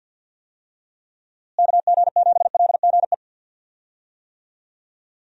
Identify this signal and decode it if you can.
Morse code, heard as a keyed tone, about 52 wpm, 700 Hz. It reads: KG7LGE
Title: USA Call Signs - 6 Characters - Repeated 3x - 50wpm